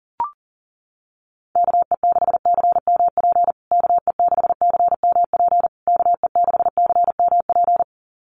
Morse code, heard as a keyed tone, about 40 wpm, 700 Hz. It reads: KE6CMP KE6CMP KE6CMP
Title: USA Call Signs Repeated 3x 40wpm